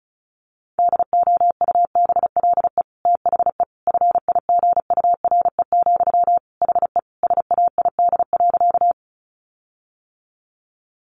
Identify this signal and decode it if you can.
Morse code, heard as a keyed tone, about 34 wpm, 700 Hz. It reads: DOUBLE THE FIGURE, HE SAID.
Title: Sentences from Top 1400 Words 35wpm